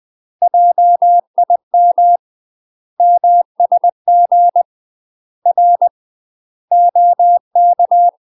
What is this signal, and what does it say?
Morse code, heard as a keyed tone, about 20 wpm, 700 Hz. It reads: JIM MSG R OK
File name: UK MASTER QSOs TRAINING / 20WPM / UK MASTER QSOs 0063